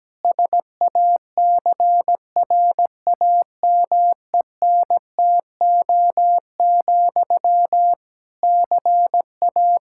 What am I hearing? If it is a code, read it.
Morse code, heard as a keyed tone, about 17 wpm, 700 Hz. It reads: SACRAMENTO, CA
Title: Practice QSO - Exchange 1 - Beginning - 17wpm